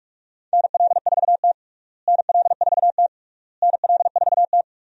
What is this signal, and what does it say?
Morse code, heard as a keyed tone, about 46 wpm, 700 Hz. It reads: NL4T NL4T NL4T
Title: USA Call Signs - 4 Characters - Repeated 3x - 45wpm